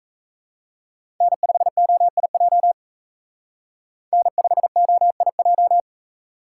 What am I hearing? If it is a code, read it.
Morse code, heard as a keyed tone, about 32 wpm, 700 Hz. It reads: N5OIJ N5OIJ
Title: Speed Racing - USA 1x3 Call Signs 28wpm